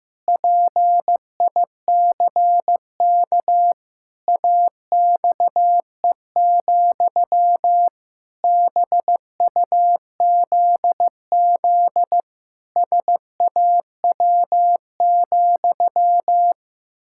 Morse code, heard as a keyed tone, about 15 wpm, 700 Hz. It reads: PICK AXE, BUZZ SAW,
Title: Pangram Challenge v2 - 15wpm